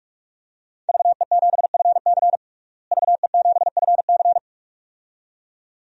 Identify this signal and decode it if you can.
Morse code, heard as a keyed tone, about 41 wpm, 700 Hz. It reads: VE7FC VE7FC
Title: DXCC-100 Call Signs - 5 Characters - Repeated 2x - 45wpm